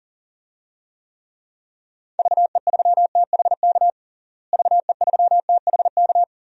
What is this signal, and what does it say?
Morse code, heard as a keyed tone, about 40 wpm, 700 Hz. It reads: VE3THK VE3THK
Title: DXCC-100 Call Signs - 6 Characters - Repeated 3x - 40wpm